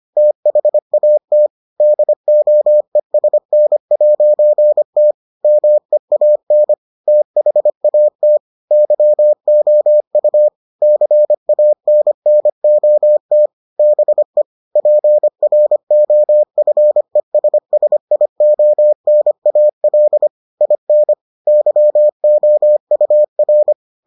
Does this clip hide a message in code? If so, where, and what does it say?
Yes: Morse code from 0.2 to 23.7 s, heard as a keyed tone, about 25 wpm, 600 Hz. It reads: THAT DOESN'T MEAN THAT YOU CANNOT BE PROFESSIONAL IN YOUR